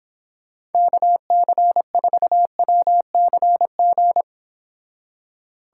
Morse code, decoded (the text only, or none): KC4WCG